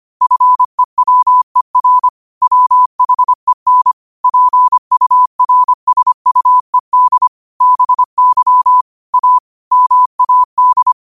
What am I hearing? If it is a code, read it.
Morse code, heard as a keyed tone, about 25 wpm, 1000 Hz. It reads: FEWER WHEN PURSUED BY A MAD